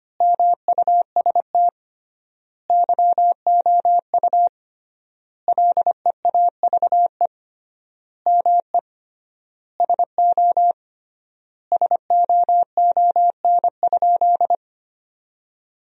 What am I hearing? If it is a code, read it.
Morse code, heard as a keyed tone, about 25 wpm, 700 Hz. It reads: MUST YOU LEAVE ME SO SOON?
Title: 3x Word-Spacing - Sentences from Top 400 Words - 25wpm